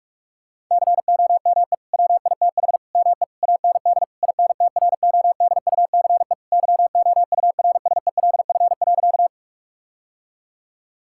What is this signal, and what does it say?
Morse code, heard as a keyed tone, about 46 wpm, 700 Hz. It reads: COME WITH ME AND INTRODUCE YOURSELF.
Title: Sentences from Top 1600 Words 45wpm